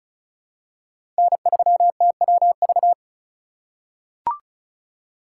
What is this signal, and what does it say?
Morse code, heard as a keyed tone, about 36 wpm, 700 Hz. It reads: N3TWV
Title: USA Call Signs - 5 Characters - 3x Character Spacing - 35wpm